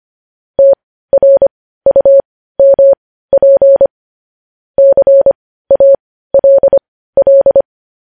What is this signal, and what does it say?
Morse code, heard as a keyed tone, about 25 wpm, spaced as for 15 wpm, 550 Hz. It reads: TRUMP CALL
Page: news headlines in morse code at 15 wpm